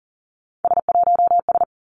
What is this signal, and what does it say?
Morse code, heard as a keyed tone, about 42 wpm, 700 Hz. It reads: S1S
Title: Three Challenging Letter-Number Combinations 40wpm